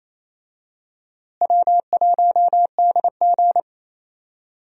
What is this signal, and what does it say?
Morse code, heard as a keyed tone, about 28 wpm, 700 Hz. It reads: W1DG
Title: USA Call Signs - 4 Characters - Repeated 3x - 28wpm